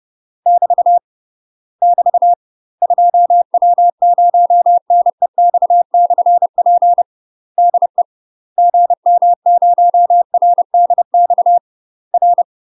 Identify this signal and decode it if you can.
Morse code, heard as a keyed tone, about 30 wpm, 700 Hz. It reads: <BT> <BT> 2W0NEX/P DE GM0RDX R
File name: UK MASTER QSOs TRAINING / 30WPM / UK MASTER QSOs 0011